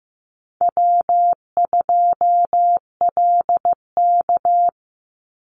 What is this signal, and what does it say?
Morse code, heard as a keyed tone, about 15 wpm, 700 Hz. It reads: W2LK
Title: Easy Call Signs - 15wpm